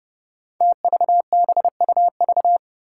Morse code, heard as a keyed tone, about 30 wpm, 700 Hz. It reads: TVBUV
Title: Five Challenging Letter-Number Combinations 30wpm